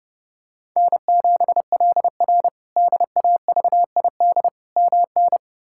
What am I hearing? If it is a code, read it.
Morse code, heard as a keyed tone, about 30 wpm, 700 Hz. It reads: N7LR DAVID MN
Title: K1USN SST QSO Practice - 30wpm